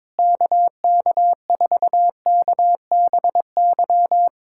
Morse code, heard as a keyed tone, about 22 wpm, 700 Hz. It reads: KK4KBY
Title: USA Callsign DE Callsign - 6 Characters - 22wpm